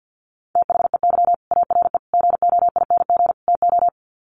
Morse code, heard as a keyed tone, about 50 wpm, 700 Hz. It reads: THEY ARE GOING TO
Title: Sets of 4 Words 50wpm